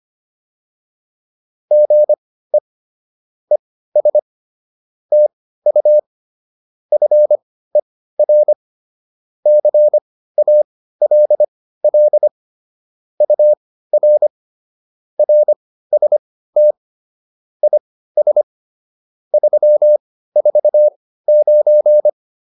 Morse code, heard as a keyed tone, about 25 wpm, spaced as for 15 wpm, 600 Hz. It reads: GE ES TU FER CALL UR RST IS 349
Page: intermediate-practice-files